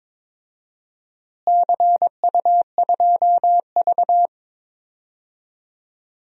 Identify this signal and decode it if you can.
Morse code, heard as a keyed tone, about 22 wpm, 700 Hz. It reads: CU2V
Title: DXCC-100 Call Signs - 4 Characters - Repeated 2x - 22wpm